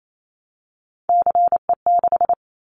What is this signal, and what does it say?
Morse code, heard as a keyed tone, about 28 wpm, 700 Hz. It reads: CE6
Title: Callsign Prefixes for Top 100 DXCC - 28wpm